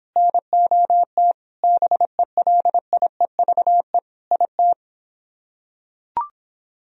Morse code, heard as a keyed tone, about 25 wpm, 700 Hz. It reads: NOT BELIEVE IT